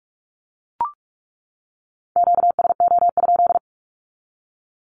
Morse code, heard as a keyed tone, about 44 wpm, 700 Hz. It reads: QSO?
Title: Q-Codes 45wpm